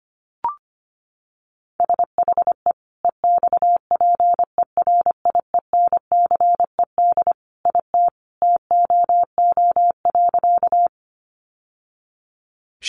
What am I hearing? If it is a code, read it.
Morse code, heard as a keyed tone, about 25 wpm, 700 Hz. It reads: SHE EXPERIENCED IT TOO.
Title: Sentences from Top 1100 Words 25wpm